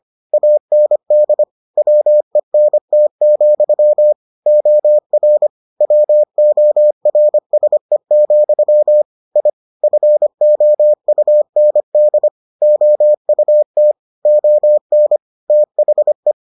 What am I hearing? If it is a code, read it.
Morse code, heard as a keyed tone, about 25 wpm, 600 Hz. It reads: AND WENT, OR WORSE, I FOUND OUT ON THE